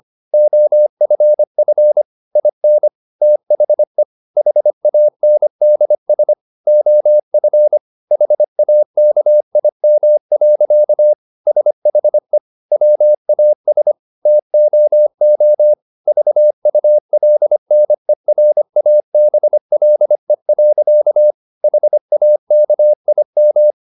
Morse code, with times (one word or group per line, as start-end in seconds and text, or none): 0.3-2.0 OFF
2.4-2.9 IN
3.2-4.0 THE
4.4-6.3 HANDS
6.7-7.8 OF
8.1-11.1 HAKIM.
11.5-12.4 SHE
12.7-13.9 WAS
14.3-15.7 TOO
16.1-21.3 VULNERABLE.
21.6-23.7 HAKIM